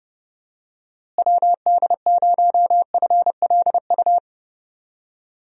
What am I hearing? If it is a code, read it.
Morse code, heard as a keyed tone, about 31 wpm, 700 Hz. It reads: WD0FLU
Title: USA Call Signs - 6 Characters - Repeated 3x - 30wpm